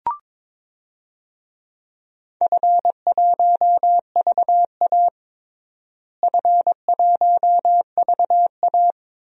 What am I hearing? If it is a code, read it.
Morse code, heard as a keyed tone, about 22 wpm, 700 Hz. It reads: F1VA F1VA
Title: DXCC-100 Call Signs - 4 Characters - Repeated 3x - 22wpm